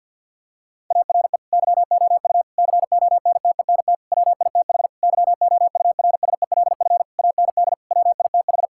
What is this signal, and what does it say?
Morse code, heard as a keyed tone, about 50 wpm, 700 Hz. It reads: ARE YOU CONTENT WITH YOURSELF AND WITH
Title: Sentences from Top 1000 Words - Review Part 6 - 50wpm